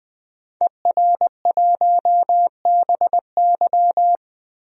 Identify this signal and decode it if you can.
Morse code, heard as a keyed tone, about 20 wpm, 700 Hz. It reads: ER1BY
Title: DXCC-100 Call Signs - 5 Characters - Repeated 2x - 20wpm